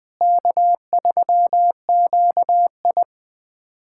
Morse code, heard as a keyed tone, about 21 wpm, 700 Hz. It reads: K3QI
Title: USA Callsign DE Callsign - 4 Characters - 20wpm